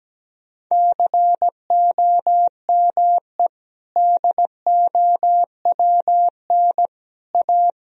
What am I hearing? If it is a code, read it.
Morse code, heard as a keyed tone, about 17 wpm, 700 Hz. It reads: COME DOWN A
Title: Sentences from Top 1000 Words - Review Part 2 - 17wpm